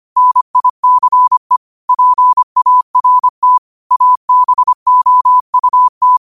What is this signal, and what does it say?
Morse code, heard as a keyed tone, about 25 wpm, 1000 Hz. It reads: NICE PART ABOUT
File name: qotdfiles / 25wpm